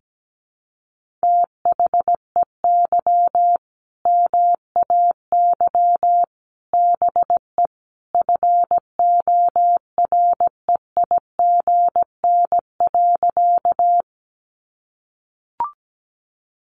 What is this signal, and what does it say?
Morse code, heard as a keyed tone, about 17 wpm, 700 Hz. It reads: THEY MAY BE FOREIGN.
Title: Sentences from Top 1400 Words 17wpm